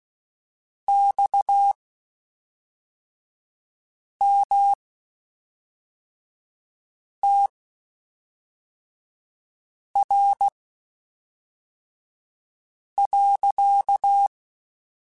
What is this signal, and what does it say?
Morse code, heard as a keyed tone, about 16 wpm, spaced as for 3 wpm, 800 Hz. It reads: XMTR.
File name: (spaced) / QSOs Spaced-mp3s / WA7QOX 1 @5 sp